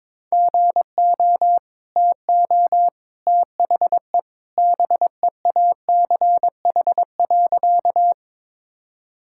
Morse code, heard as a keyed tone, about 22 wpm, 700 Hz. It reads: GO TO THE BEACH.
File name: practice / 22wpm / Easy Sentences with Hints - 22wpm